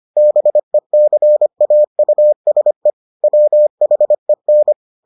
Morse code, heard as a keyed tone, about 25 wpm, 600 Hz. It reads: BECAUSE WHEN